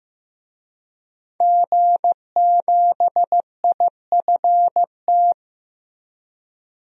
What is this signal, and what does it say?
Morse code, heard as a keyed tone, about 15 wpm, 700 Hz. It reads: G7IFT
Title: DXCC-100 Call Signs - 5 Characters - Repeated 3x - 15wpm